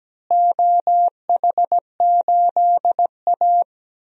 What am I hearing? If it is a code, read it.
Morse code, heard as a keyed tone, about 17 wpm, 700 Hz. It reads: OH8A
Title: DXCC-100 Call Signs - 4 Characters - Repeated 2x - 17wpm